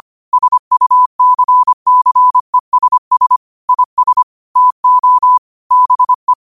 Morse code, heard as a keyed tone, about 25 wpm, 1000 Hz. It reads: SUCCESS IS TO BE
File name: qotdfiles / 25wpm